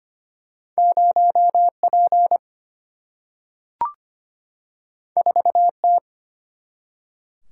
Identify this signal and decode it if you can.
Morse code, heard as a keyed tone, about 25 wpm, 700 Hz. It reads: 0P 4T
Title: Single-Letter Single-Number 25wpm